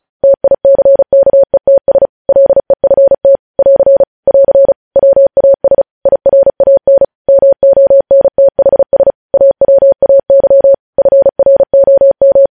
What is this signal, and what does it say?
Morse code, heard as a keyed tone, about 36 wpm, 550 Hz. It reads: TICKETS LEFT <AR> <AR> WAS IRAN MONTHS AWAY FROM
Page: news headlines in morse code at 35 wpm